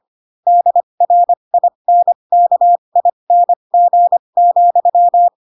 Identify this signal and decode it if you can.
Morse code, heard as a keyed tone, about 25 wpm, spaced as for 22 wpm, 700 Hz. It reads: DRINKING,
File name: Pinocchio / Pinocchio ch4 18 25 ik1hge